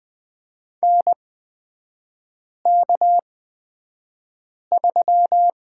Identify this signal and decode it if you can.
Morse code, heard as a keyed tone, about 20 wpm, spaced as for 5 wpm, 700 Hz. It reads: NK3
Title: 236 Lesson 38e - Letter q - Three Characters 20wpm